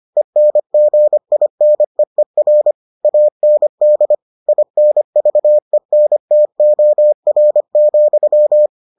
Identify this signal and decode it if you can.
Morse code, heard as a keyed tone, about 25 wpm, 600 Hz. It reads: ENGINEER AND INVENTOR,